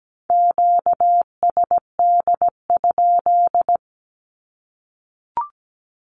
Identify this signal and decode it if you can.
Morse code, heard as a keyed tone, about 17 wpm, 700 Hz. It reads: QSD?
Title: Q-Codes Repeated 3x 17wpm